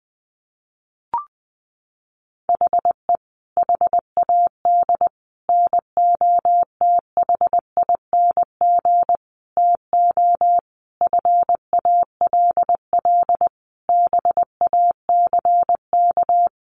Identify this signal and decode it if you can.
Morse code, heard as a keyed tone, about 20 wpm, 700 Hz. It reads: HE HAD NOTHING TO FALL BACK
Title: Sentences from Top 1100 Words 20wpm